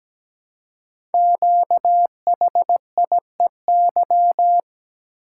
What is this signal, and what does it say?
Morse code, heard as a keyed tone, about 17 wpm, 700 Hz. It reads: QHIEY
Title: Code Groups - 5 Letters - 17wpm